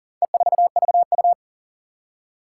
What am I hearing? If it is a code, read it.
Morse code, heard as a keyed tone, about 40 wpm, 700 Hz. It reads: E4VU
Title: Four Challenging Letter-Number Combinations 40wpm